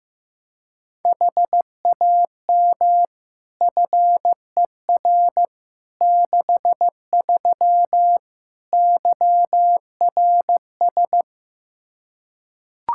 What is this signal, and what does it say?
Morse code, heard as a keyed tone, about 16 wpm, 700 Hz. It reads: HAM FER 63 YRS
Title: Practice QSO - Exchange 3 - More Info - 15wpm